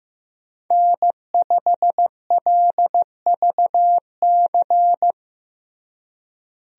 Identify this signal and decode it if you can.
Morse code, heard as a keyed tone, about 15 wpm, 700 Hz. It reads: N5LVC
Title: USA Call Signs - 5 Characters - Repeated 2x - 15wpm